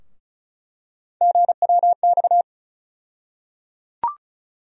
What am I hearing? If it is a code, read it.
Morse code, heard as a keyed tone, about 34 wpm, 700 Hz. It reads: GWX
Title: USA Call Sign Postfixes - 3 Letters 35wpm